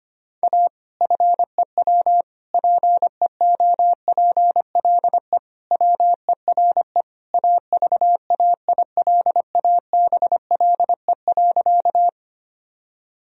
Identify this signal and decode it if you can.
Morse code, heard as a keyed tone, about 25 wpm, 700 Hz. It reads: A FEW PEOPLE WERE AVAILABLE.
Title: Sentences from Top 1000 Words - Review Part 3 - 25wpm